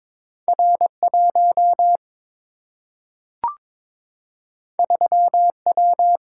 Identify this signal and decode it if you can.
Morse code, heard as a keyed tone, about 22 wpm, 700 Hz. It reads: R 1 3 W
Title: Single-Letter Single-Number 22wpm